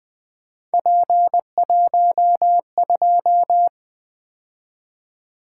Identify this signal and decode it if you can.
Morse code, heard as a keyed tone, about 20 wpm, 700 Hz. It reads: P12
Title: Three Letter-Number Combinations - 20wpm